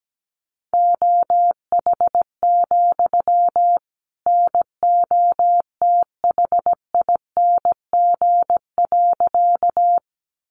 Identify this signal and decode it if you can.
Morse code, heard as a keyed tone, about 17 wpm, 700 Hz. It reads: OH, NOTHING.
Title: Sentences from Top 500 Words 17wpm